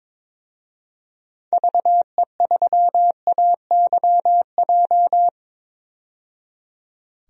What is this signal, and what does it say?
Morse code, heard as a keyed tone, about 22 wpm, 700 Hz. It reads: VE3AYJ
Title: DXCC-100 Call Signs - 6 Characters - 22wpm